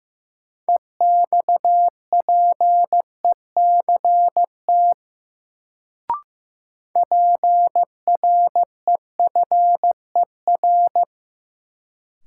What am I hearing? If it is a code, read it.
Morse code, heard as a keyed tone, about 16 wpm, 700 Hz. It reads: EXPECT PREFER
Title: Sentences from Top 1100 Words - New Word Drill - 15wpm